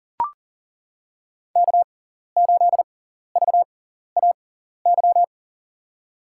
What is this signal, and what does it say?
Morse code, heard as a keyed tone, about 40 wpm, spaced as for 14 wpm, 700 Hz. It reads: K8VAY
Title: USA Call Signs - 5 Characters - 6x Character Spacing - 40wpm